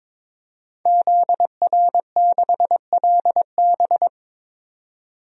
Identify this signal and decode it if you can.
Morse code, heard as a keyed tone, about 21 wpm, 700 Hz. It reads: ZR6LB
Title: DXCC-100 Call Signs - 5 Characters - 22wpm